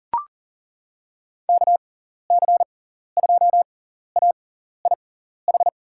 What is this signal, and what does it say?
Morse code, heard as a keyed tone, about 37 wpm, spaced as for 14 wpm, 700 Hz. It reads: KC2AIH
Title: USA Call Signs - 6 Characters - 6x Character Spacing - 40wpm